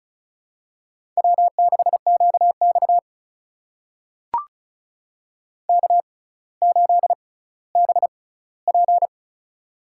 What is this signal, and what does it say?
Morse code, heard as a keyed tone, about 34 wpm, spaced as for 18 wpm, 700 Hz. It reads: W6QX K8BP
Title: USA Call Signs - 4 Characters - 6x Character Spacing - 35wpm